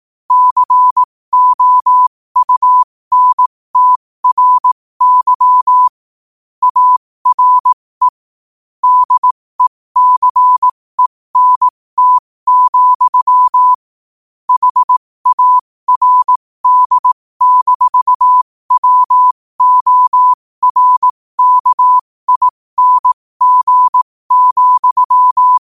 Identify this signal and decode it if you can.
Morse code, heard as a keyed tone, about 18 wpm, spaced as for 15 wpm, 1000 Hz. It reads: COUNTRY ARE DECENT, HARD-WORKING,
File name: qotdfiles / 15wpm